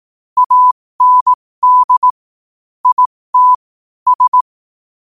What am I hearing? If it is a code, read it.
Morse code, heard as a keyed tone, about 18 wpm, spaced as for 15 wpm, 1000 Hz. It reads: AND IT S